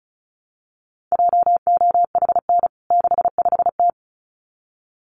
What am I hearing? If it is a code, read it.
Morse code, heard as a keyed tone, about 34 wpm, 700 Hz. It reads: JOHN 65T
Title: CWT Exchange 35wpm